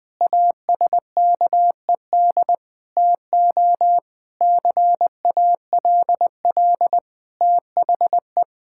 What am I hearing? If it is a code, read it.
Morse code, heard as a keyed tone, about 20 wpm, 700 Hz. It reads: ASKED TO CALL THE
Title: Sentences with 7-Word Phrases - 20wpm